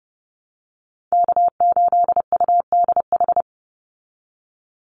Morse code, heard as a keyed tone, about 30 wpm, 700 Hz. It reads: K8UDH
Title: Call Signs Repeated 3x - 30wpm